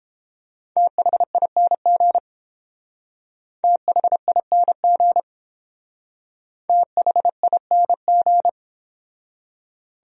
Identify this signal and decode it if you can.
Morse code, heard as a keyed tone, about 28 wpm, 700 Hz. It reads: THING THING THING